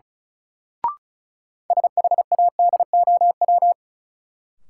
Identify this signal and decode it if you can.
Morse code, heard as a keyed tone, about 35 wpm, 700 Hz. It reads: SHADOW
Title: Sets of 5 Words Encore - New Word Drill - 35wpm